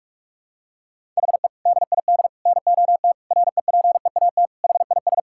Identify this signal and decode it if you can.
Morse code, heard as a keyed tone, about 46 wpm, 700 Hz. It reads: HE DID NOT REPEAT HIS
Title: Sentences from Top 1600 Words 45wpm